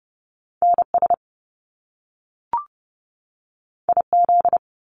Morse code, heard as a keyed tone, about 30 wpm, 700 Hz. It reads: NS IZ
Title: Top Two Letter Combinations in Words 30wpm